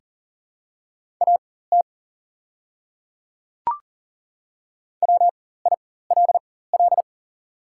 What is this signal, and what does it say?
Morse code, heard as a keyed tone, about 40 wpm, spaced as for 19 wpm, 690 Hz. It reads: AT WILL
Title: 4x Character Spacing - Top 100 Words - 40wpm